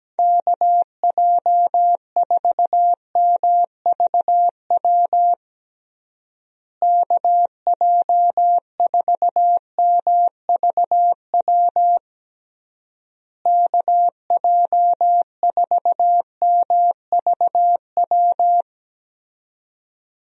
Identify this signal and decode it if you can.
Morse code, heard as a keyed tone, about 17 wpm, 700 Hz. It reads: KJ4MVW KJ4MVW KJ4MVW
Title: USA Call Signs - 6 Characters - Repeated 3x - 17wpm